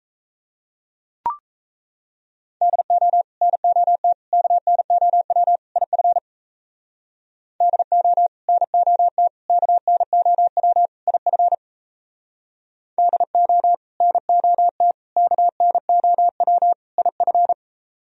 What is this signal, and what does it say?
Morse code, heard as a keyed tone, about 37 wpm, 700 Hz. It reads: DO NOT KNOW IF DO NOT KNOW IF DO NOT KNOW IF